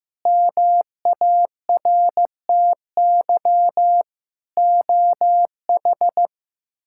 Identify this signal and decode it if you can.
Morse code, heard as a keyed tone, about 15 wpm, 700 Hz. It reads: MARTY OH
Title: K1USN SST QSO Practice - 15wpm